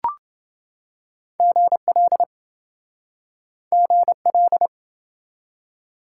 Morse code, heard as a keyed tone, about 26 wpm, 700 Hz. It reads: GL GL